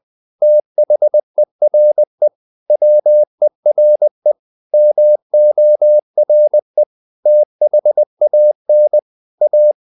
Morse code, heard as a keyed tone, about 20 wpm, 600 Hz. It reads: THERE WERE MORE THAN A
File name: the old man and the sea part 2 20wpm